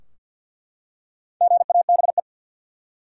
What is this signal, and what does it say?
Morse code, heard as a keyed tone, about 51 wpm, 700 Hz. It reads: GABE
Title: USA Common Names - Max 5 Letters - 50wpm